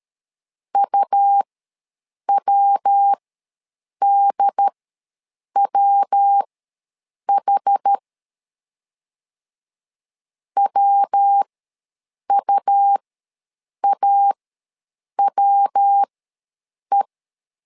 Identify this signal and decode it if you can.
Morse code, heard as a keyed tone, about 13 wpm, spaced as for 7 wpm, 800 Hz. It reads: UWDWH WUAWE
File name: LEZIONI / 3 MWRO / 3 MW ws 12wpm 1816